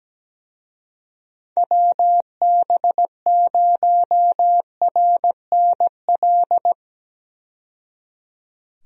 Morse code, heard as a keyed tone, about 17 wpm, 700 Hz. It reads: WB0RNL
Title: USA Call Signs - 6 Characters - 17wpm